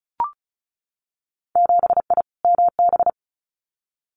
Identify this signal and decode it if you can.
Morse code, heard as a keyed tone, about 36 wpm, 700 Hz. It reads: 7I MB
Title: Winter Field Day Exchanges Encore 35wpm